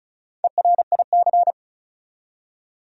Morse code, heard as a keyed tone, about 36 wpm, 700 Hz. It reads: ERIC
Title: USA Common Names - Max 4 Letters - 35wpm